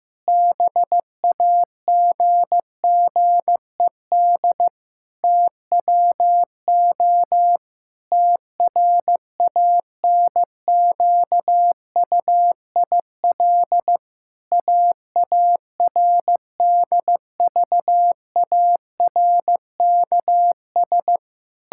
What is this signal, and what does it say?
Morse code, heard as a keyed tone, about 16 wpm, 700 Hz. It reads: BAGGED TWO TRANQUIL AARDVARKS
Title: Pangram Challenge v2 - 15wpm